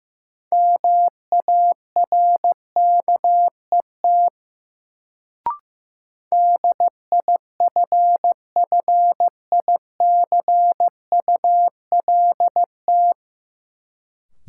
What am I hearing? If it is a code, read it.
Morse code, heard as a keyed tone, about 15 wpm, 700 Hz. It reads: MARKET DIFFICULT
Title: Sentences from Top 600 Words - New Word Drill - 15wpm